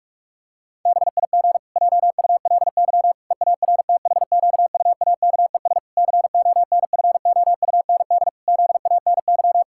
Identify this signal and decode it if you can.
Morse code, heard as a keyed tone, about 46 wpm, 700 Hz. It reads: BIG JULY EARTHQUAKES CONFOUND ZANY
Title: Pangram Challenge v2 - 45wpm